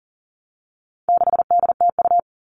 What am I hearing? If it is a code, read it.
Morse code, heard as a keyed tone, about 40 wpm, 700 Hz. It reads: 6DTU
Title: All Cut Numbers - 4-Digit 40wpm